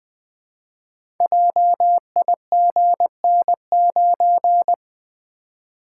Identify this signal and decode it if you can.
Morse code, heard as a keyed tone, about 20 wpm, 700 Hz. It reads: JIGN9
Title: Code Groups - 5 Letter:Numbers - 20wpm